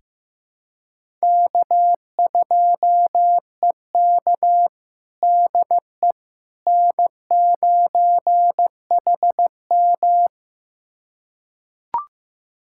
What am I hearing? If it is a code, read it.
Morse code, heard as a keyed tone, about 15 wpm, 700 Hz. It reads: K2EK DE N9HM
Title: USA Callsign DE Callsign - 4 Characters - 15wpm